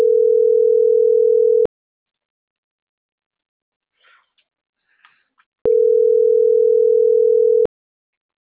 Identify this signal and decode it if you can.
Ringback tone (North America), 440 + 480 Hz.